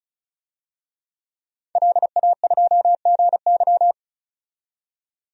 Morse code, heard as a keyed tone, about 36 wpm, 700 Hz. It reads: LA2GY
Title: DXCC-100 Call Signs - 5 Characters - 35wpm